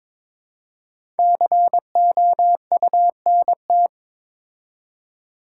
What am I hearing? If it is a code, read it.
Morse code, heard as a keyed tone, about 22 wpm, 700 Hz. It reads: COUNT
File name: 5 Letter Words - First 3 Letters Given - 22wpm